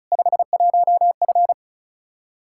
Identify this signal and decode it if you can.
Morse code, heard as a keyed tone, about 34 wpm, 700 Hz. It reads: 51F
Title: Practice QSO - Exchange 2 - Info - 35wpm